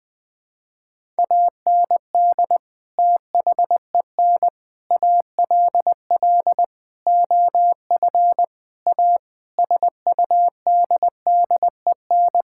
Morse code, heard as a keyed tone, about 20 wpm, 700 Hz. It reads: AND THEN ALL OF A SUDDEN
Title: Sets of 6 Words - 20wpm